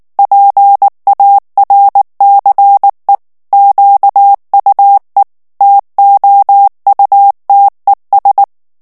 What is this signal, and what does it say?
Morse code, heard as a keyed tone, about 19 wpm, 800 Hz. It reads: PARCE QUE TOUTES
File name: morse mp3 / FRANCAIS TEXTE CLAIR 1 VITESSE 18 MOTS MINUTES